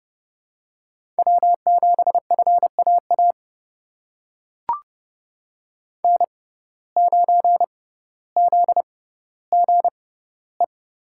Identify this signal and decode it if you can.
Morse code, heard as a keyed tone, about 30 wpm, spaced as for 15 wpm, 700 Hz. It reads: W7FAA N9ZGE
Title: USA Call Signs - 5 Characters - 6x Character Spacing - 30wpm